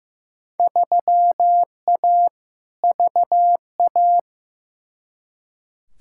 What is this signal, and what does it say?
Morse code, heard as a keyed tone, about 15 wpm, 700 Hz. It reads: 3A VA
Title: Field Day Exchanges Encore - 15wpm